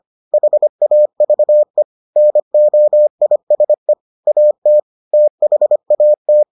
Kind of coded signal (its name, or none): Morse code